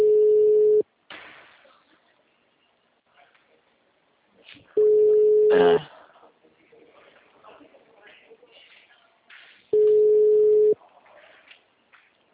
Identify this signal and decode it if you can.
Ringback tone (Europe, 425 Hz), with speech.